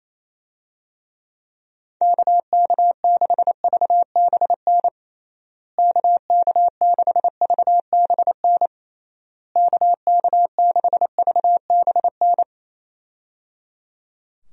Morse code, heard as a keyed tone, about 27 wpm, 700 Hz. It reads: KK6VBN KK6VBN KK6VBN
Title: USA Call Signs - 6 Characters - Repeated 3x - 28wpm